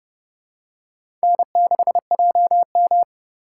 Morse code, heard as a keyed tone, about 31 wpm, 700 Hz. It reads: N6JM
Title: USA Call Signs - 4 Characters - 2x Character Spacing - 30wpm